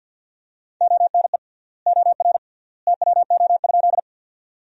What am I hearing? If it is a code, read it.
Morse code, heard as a keyed tone, about 50 wpm, 700 Hz. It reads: ONE OR TWO?
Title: 3x Word-Spacing - Sentences from Top 100 Words - 50wpm